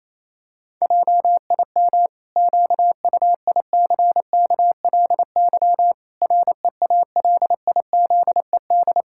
Morse code, heard as a keyed tone, about 28 wpm, 700 Hz. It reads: JIM QUICKLY REALIZED
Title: Pangram Challenge v2 - 28wpm